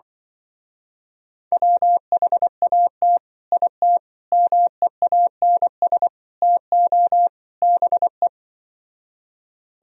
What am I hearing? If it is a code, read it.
Morse code, heard as a keyed tone, about 24 wpm, 700 Hz. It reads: WHAT IT MEANS TO BE
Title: Speed-Racing - Sets of 5 Words 20wpm